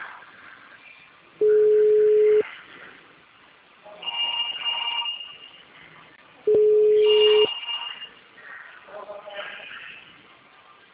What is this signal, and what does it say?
Ringback tone (Europe, 425 Hz).